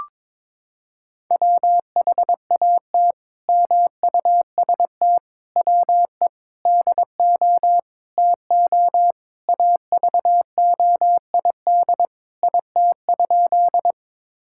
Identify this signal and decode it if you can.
Morse code, heard as a keyed tone, about 22 wpm, 700 Hz. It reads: WHAT MUST WE DO TO AVOID IT?
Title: Sentences from Top 1000 Words - Review Part 6 - 22wpm